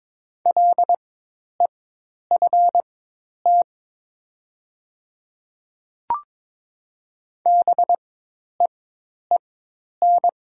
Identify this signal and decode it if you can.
Morse code, heard as a keyed tone, about 22 wpm, spaced as for 10 wpm, 700 Hz. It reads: LEFT BEEN
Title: 4x Character Spacing - Top 300 Words - 22wpm